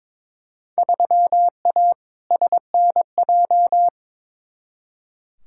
Morse code, heard as a keyed tone, about 21 wpm, 700 Hz. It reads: 3A SNJ
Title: Field Day Exchanges - 22wpm